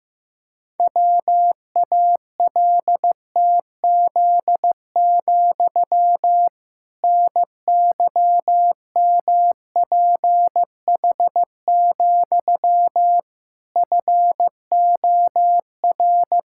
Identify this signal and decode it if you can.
Morse code, heard as a keyed tone, about 15 wpm, 700 Hz. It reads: WALTZ, NYMPH, FOR
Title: Pangram Challenge v2 - 15wpm